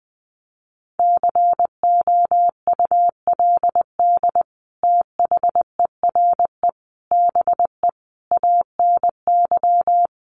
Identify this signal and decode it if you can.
Morse code, heard as a keyed tone, about 20 wpm, 700 Hz. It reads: COULD THERE BE ANY
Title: Sentences from Top 1500 Words 20wpm